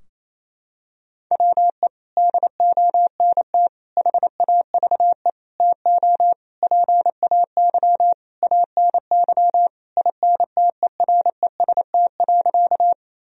Morse code, heard as a keyed tone, about 28 wpm, 700 Hz. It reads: WE DONT HAVE TO PAY ANY INTEREST.
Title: Sentences from Top 1000 Words - Review Part 2 - 28wpm